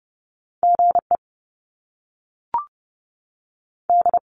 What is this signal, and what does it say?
Morse code, heard as a keyed tone, about 30 wpm, 700 Hz. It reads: G E D
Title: Top Two Letter Combinations in Words 30wpm